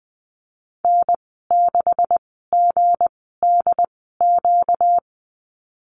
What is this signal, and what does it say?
Morse code, heard as a keyed tone, about 20 wpm, spaced as for 14 wpm, 700 Hz. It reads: N6GDQ
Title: 245 Lesson 39g - Letter z - Call Signs 20wpm